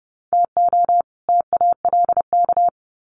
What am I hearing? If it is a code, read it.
Morse code, heard as a keyed tone, about 30 wpm, 700 Hz. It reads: TO TALK
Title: Sets of 2 Words 30wpm